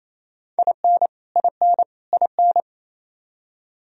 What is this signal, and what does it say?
Morse code, heard as a keyed tone, about 28 wpm, 700 Hz. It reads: IN IN IN
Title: Top 100 Words Repeated 28wpm